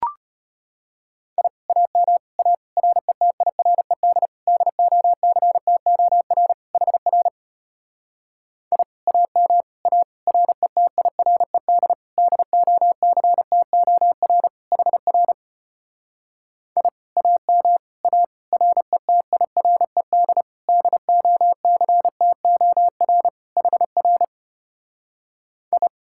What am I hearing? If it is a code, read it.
Morse code, heard as a keyed tone, about 33 wpm, 700 Hz. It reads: I AM A RETIRED DOCTOR HR I AM A RETIRED DOCTOR HR I AM A RETIRED DOCTOR HR I